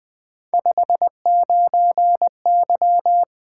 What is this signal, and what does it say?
Morse code, heard as a keyed tone, about 20 wpm, 700 Hz. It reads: 59Y
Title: Three Letter-Number Combinations - 20wpm